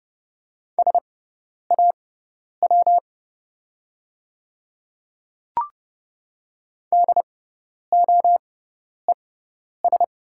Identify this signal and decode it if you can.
Morse code, heard as a keyed tone, about 30 wpm, spaced as for 10 wpm, 700 Hz. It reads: SAW DOES
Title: 6x Character Spacing - Top 300 Words - 30wpm